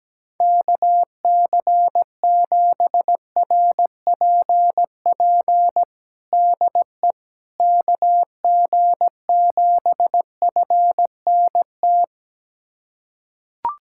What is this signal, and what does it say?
Morse code, heard as a keyed tone, about 17 wpm, 700 Hz. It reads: KC7RPP DE KG7FNT
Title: USA Callsign DE Callsign - 6 Characters - 17wpm